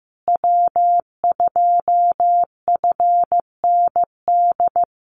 Morse code, heard as a keyed tone, about 15 wpm, 700 Hz. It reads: W2FND
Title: USA Call Signs Repeated 2x 15wpm